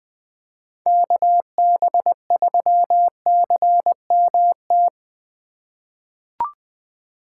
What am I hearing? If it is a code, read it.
Morse code, heard as a keyed tone, about 20 wpm, 700 Hz. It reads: KB3CMT
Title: 2023 POTA Activator Call Signs - 20wpm